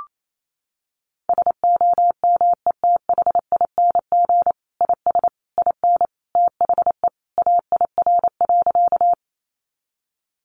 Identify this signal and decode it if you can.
Morse code, heard as a keyed tone, about 28 wpm, 700 Hz. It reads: SOMETHING IS IN THE AIR.
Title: Sentences from Top 600 Words 28wpm